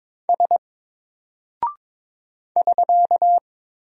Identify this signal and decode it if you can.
Morse code, heard as a keyed tone, about 22 wpm, 700 Hz. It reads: S <SK>
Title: Single Character (Letter:Number:Punctuation:Prosign) Alt 22wpm